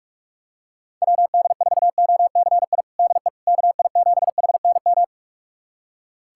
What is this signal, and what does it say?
Morse code, heard as a keyed tone, about 42 wpm, 700 Hz. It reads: WD4OCI DE KI7HNM
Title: USA Callsign DE Callsign - 6 Characters - 45wpm